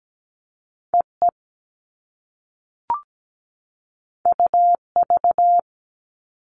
Morse code, heard as a keyed tone, about 17 wpm, 700 Hz. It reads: EE UV
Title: All Cut Numbers - 2-Digit 17wpm